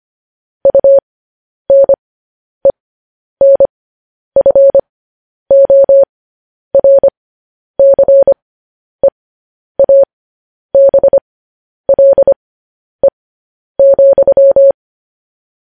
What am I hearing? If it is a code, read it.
Morse code, heard as a keyed tone, about 25 wpm, spaced as for 10 wpm, 550 Hz. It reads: UNENFORCEABLE,
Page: News Headlines in Morse Code at 10 WPM